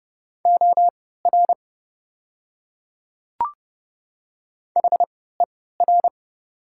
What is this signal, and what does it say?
Morse code, heard as a keyed tone, about 30 wpm, spaced as for 17 wpm, 700 Hz. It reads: OR HER
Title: 3x Character Spacing - Top 100 Words - 30wpm